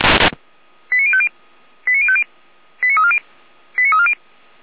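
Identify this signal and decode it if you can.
Five-tone selective call (ZVEI), heard as quick runs of five tones, tones of about 70 ms. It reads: 89050 89050 89340 89340
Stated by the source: Five-tone ZVEI selective call received on SDR.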